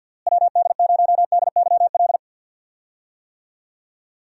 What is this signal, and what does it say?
Morse code, heard as a keyed tone, about 47 wpm, 700 Hz. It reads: WD0DYL
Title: USA Call Signs - 6 Characters - Repeated 2x - 50wpm